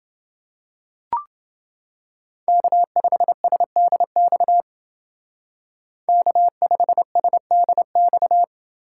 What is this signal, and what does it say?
Morse code, heard as a keyed tone, about 29 wpm, 700 Hz. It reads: K5SDX K5SDX